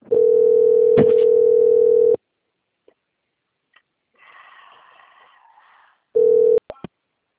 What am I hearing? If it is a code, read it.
Ringback tone (North America), 440 + 480 Hz.
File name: out- - - - - .11525